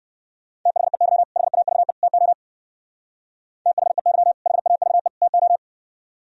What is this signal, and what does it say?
Morse code, heard as a keyed tone, about 55 wpm, 700 Hz. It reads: THEY HAVE TO THEY HAVE TO
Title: Speed-Racing - Sets of 3 Words 45wpm